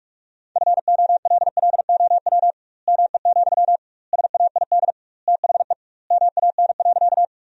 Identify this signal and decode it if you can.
Morse code, heard as a keyed tone, about 46 wpm, 700 Hz. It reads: FOLLOW ME, SAID THE MAN.
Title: 2x Word-Spacing - Sentences from Top 200 Words - 45wpm